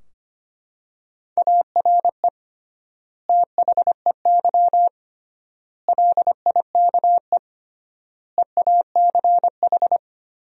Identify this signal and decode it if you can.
Morse code, heard as a keyed tone, about 25 wpm, 700 Hz. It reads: ARE THEY LIKE EACH
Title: 3x Word-Spacing - Sentences from Top 200 Words - 25wpm